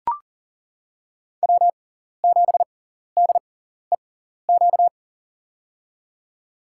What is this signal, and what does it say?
Morse code, heard as a keyed tone, about 39 wpm, spaced as for 14 wpm, 700 Hz. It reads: W7DEQ
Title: USA Call Signs - 5 Characters - 6x Character Spacing - 40wpm